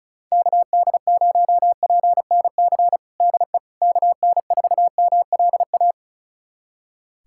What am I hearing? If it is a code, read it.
Morse code, heard as a keyed tone, about 34 wpm, 700 Hz. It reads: KD0PNC DE KN4MLA
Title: USA Callsign DE Callsign - 6 Characters - 35wpm